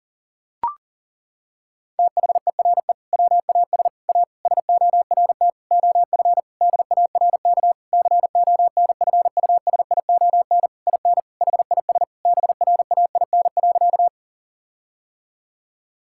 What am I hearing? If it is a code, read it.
Morse code, heard as a keyed tone, about 40 wpm, 700 Hz. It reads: THERE WAS A SORT OF DARK CONFUSION IN HIS BRAIN.
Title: Sentences from Top 1600 Words 40wpm